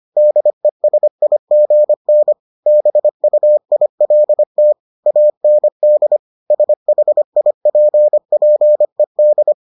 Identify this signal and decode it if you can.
Morse code, heard as a keyed tone, about 25 wpm, 600 Hz. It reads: DESIGN BUILT AND SHIPPED